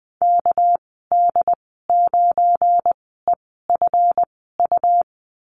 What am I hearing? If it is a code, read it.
Morse code, heard as a keyed tone, about 20 wpm, spaced as for 14 wpm, 700 Hz. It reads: KD9EFU
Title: 231 Lesson 37g - Letter x - Call Signs 20wpm